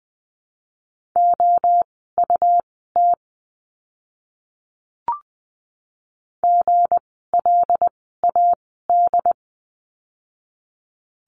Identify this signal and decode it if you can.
Morse code, heard as a keyed tone, about 20 wpm, spaced as for 15 wpm, 700 Hz. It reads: OUT GLAD
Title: 160 Lesson 26f - Letter g - Words 20wpm